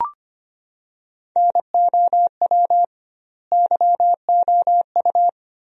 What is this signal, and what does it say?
Morse code, heard as a keyed tone, about 25 wpm, 700 Hz. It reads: NOW YOU
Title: 2x Word-Spacing - Sentences from Top 300 Words - 25wpm